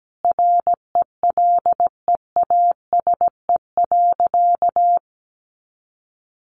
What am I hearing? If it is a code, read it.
Morse code, heard as a keyed tone, about 17 wpm, 700 Hz. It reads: RELEASE.
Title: Sentences from Top 1100 Words 17wpm